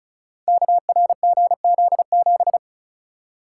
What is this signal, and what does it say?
Morse code, heard as a keyed tone, about 36 wpm, 690 Hz. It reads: KRGZ7
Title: Code Groups - 5 Letter:Numbers - 35wpm